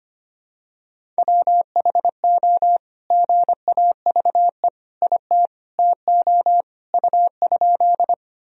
Morse code, heard as a keyed tone, about 25 wpm, 700 Hz. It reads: WHO GAVE IT TO U?
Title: Sentences from Top 500 Words and Abbreviations 25wpm